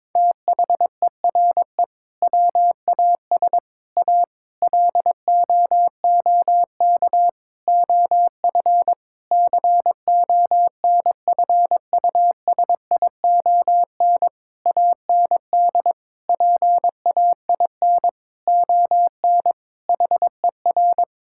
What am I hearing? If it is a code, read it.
Morse code, heard as a keyed tone, about 22 wpm, 700 Hz. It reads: THERE WAS A LOOK OF CONFUSION AND PAIN ON HER
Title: Sentences from Top 1600 Words 22wpm